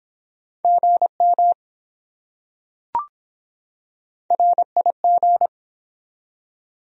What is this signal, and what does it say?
Morse code, heard as a keyed tone, about 24 wpm, 700 Hz. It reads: GM RIG